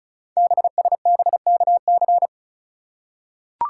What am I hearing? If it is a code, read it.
Morse code, heard as a keyed tone, about 36 wpm, 700 Hz. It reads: BSBKC
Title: Code Groups - 5 Letters - 35wpm